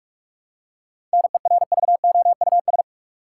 Morse code, heard as a keyed tone, about 46 wpm, 700 Hz. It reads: NERVOUS
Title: Seven Letter Words - 45wpm